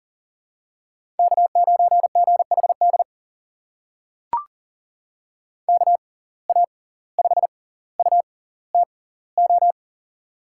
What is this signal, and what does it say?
Morse code, heard as a keyed tone, about 40 wpm, spaced as for 19 wpm, 700 Hz. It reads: K9GHD KA5UTO
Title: USA Call Signs - 6x Character Spacing - 40wpm